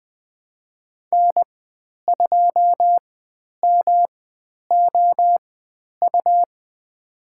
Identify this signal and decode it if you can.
Morse code, heard as a keyed tone, about 20 wpm, spaced as for 10 wpm, 700 Hz. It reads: N2MOU
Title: 109 Lesson 18g - Letter m - Call Signs 20wpm